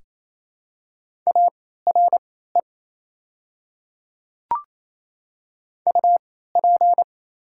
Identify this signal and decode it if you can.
Morse code, heard as a keyed tone, about 28 wpm, spaced as for 16 wpm, 700 Hz. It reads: ARE UP
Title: 3x Character Spacing - Top 100 Words - 28wpm